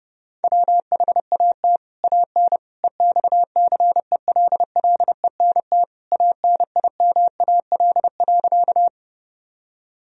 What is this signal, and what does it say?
Morse code, heard as a keyed tone, about 30 wpm, 700 Hz. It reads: WHAT AN EXCELLENT ANIMAL.
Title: Sentences from Top 1000 Words - Review Part 1 - 30wpm